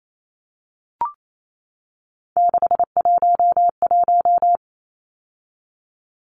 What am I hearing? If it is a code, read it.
Morse code, heard as a keyed tone, about 28 wpm, 700 Hz. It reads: 611
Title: Numbers - 3-Digit 28wpm